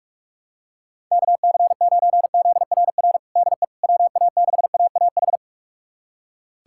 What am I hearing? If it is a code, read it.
Morse code, heard as a keyed tone, about 45 wpm, 700 Hz. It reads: KC9ZRR DE WA6AAH
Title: USA Callsign DE Callsign 45wpm